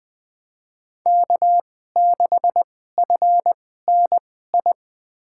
Morse code, heard as a keyed tone, about 21 wpm, spaced as for 15 wpm, 700 Hz. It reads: K6FNI
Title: USA Call Signs - 5 Characters - 2x Character Spacing - 20wpm